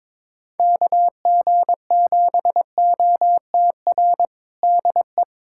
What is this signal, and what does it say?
Morse code, heard as a keyed tone, about 21 wpm, 700 Hz. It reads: KG7OTR DE
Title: USA Callsign DE Callsign - 6 Characters - 22wpm